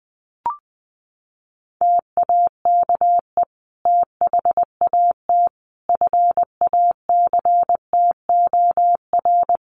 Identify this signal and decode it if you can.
Morse code, heard as a keyed tone, about 20 wpm, 700 Hz. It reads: TAKE THAT FACTOR
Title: Sentences from Top 1300 Words 20wpm